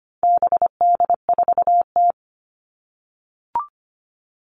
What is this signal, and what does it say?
Morse code, heard as a keyed tone, about 25 wpm, 700 Hz. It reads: BD4T
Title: All Cut Numbers - 4-Digit 25wpm